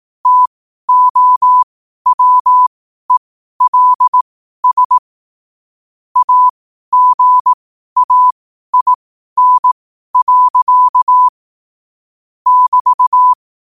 Morse code, heard as a keyed tone, about 18 wpm, spaced as for 12 wpm, 1000 Hz. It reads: TOWELS AGAIN. <BT>
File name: qotdfiles / 12wpm